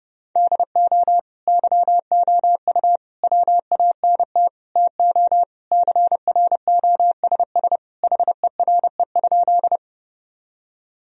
Morse code, heard as a keyed tone, about 30 wpm, 700 Hz. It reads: DO YOU WANT TO CROSS HERE?
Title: Sentences from Top 1000 Words - Review Part 1 - 30wpm